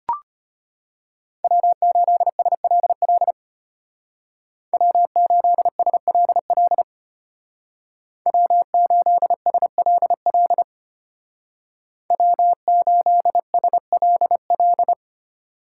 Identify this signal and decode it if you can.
Morse code, heard as a keyed tone, about 31 wpm, 700 Hz. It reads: W8SLL W8SLL W8SLL W8SLL
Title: Speed Racing - USA 1x3 Call Signs 25wpm